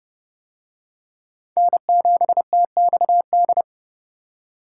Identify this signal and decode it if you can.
Morse code, heard as a keyed tone, about 26 wpm, 700 Hz. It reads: N7TXD